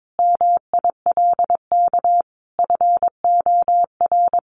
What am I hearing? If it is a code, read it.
Morse code, heard as a keyed tone, about 22 wpm, 700 Hz. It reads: MILK FOR
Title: Sentences from Top 1500 Words 22wpm